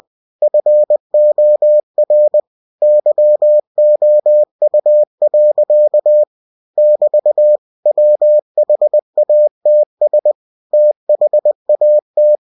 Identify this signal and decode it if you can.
Morse code, heard as a keyed tone, about 20 wpm, 600 Hz. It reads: FOR YOU. <BT> WHATS THAT
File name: cw-books / winnie-the-pooh / winnie-20wpm-page-0105